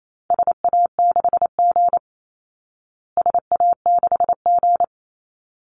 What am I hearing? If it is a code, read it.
Morse code, heard as a keyed tone, about 28 wpm, 700 Hz. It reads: SA6G SA6G
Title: Call Signs Repeated 3x - 28wpm